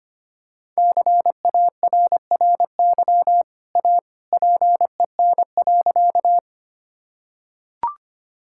Morse code, heard as a keyed tone, about 25 wpm, 700 Hz. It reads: CARRY A PEN.
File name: Easy Sentences with Hints - 25wpm